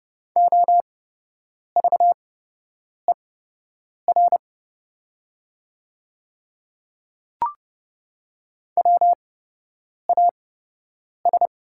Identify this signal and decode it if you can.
Morse code, heard as a keyed tone, about 30 wpm, spaced as for 8 wpm, 700 Hz. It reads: OVER WAS
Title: 8x Character Spacing - Top 100 Words - 30wpm